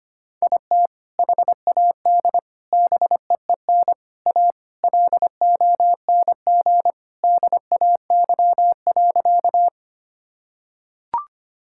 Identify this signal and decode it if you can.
Morse code, heard as a keyed tone, about 25 wpm, 700 Hz. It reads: IT HAD BEEN A LONG DAY.
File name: practice / 25wpm / Sentences with 2-Word Phrases - 25wpm